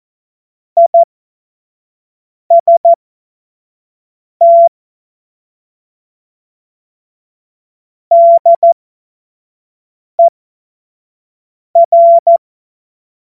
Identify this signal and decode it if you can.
Morse code, heard as a keyed tone, about 14 wpm, spaced as for 5 wpm, 680 Hz. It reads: IST DER
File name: Morsekurs / Witze / Witz 30. August 2022 25 F70